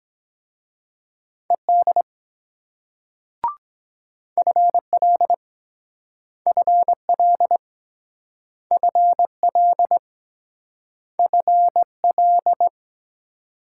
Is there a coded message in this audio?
Yes: Morse code from 1.5 to 12.7 s, heard as a keyed tone, about 23 wpm, 700 Hz. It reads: ED FL FL FL FL